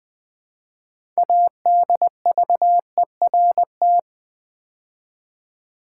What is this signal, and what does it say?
Morse code, heard as a keyed tone, about 20 wpm, 700 Hz. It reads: ADVERT
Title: Build A Word - 3 Letters At A Time - 9-Letter Words - 20wpm